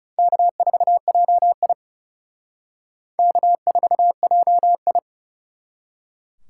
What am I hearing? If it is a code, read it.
Morse code, heard as a keyed tone, about 39 wpm, 700 Hz. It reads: K4JI K4JI